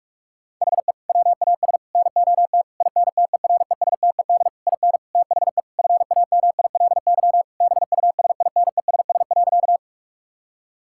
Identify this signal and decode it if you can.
Morse code, heard as a keyed tone, about 46 wpm, 700 Hz. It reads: HE WAS NOT INTERESTED IN THE FAMILY BUSINESS.
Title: Sentences from Top 1000 Words - Review Part 6 - 45wpm